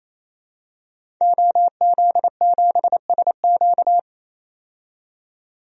Morse code, heard as a keyed tone, about 27 wpm, 700 Hz. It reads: OZ7SQ
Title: DXCC-100 Call Signs - 5 Characters - 28wpm